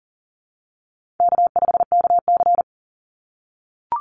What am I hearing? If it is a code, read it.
Morse code, heard as a keyed tone, about 40 wpm, 700 Hz. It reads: K5KC
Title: Easy Call Signs - 40wpm